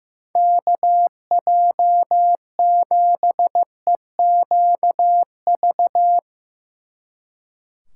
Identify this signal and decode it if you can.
Morse code, heard as a keyed tone, about 17 wpm, 700 Hz. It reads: KJ7EQV
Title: Speed Racing - USA 2x3 Call Signs 15wpm